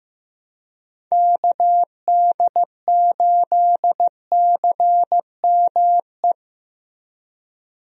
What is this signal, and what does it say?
Morse code, heard as a keyed tone, about 15 wpm, 700 Hz. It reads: KD8CME
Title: USA Call Signs - 6 Characters - Repeated 3x - 15wpm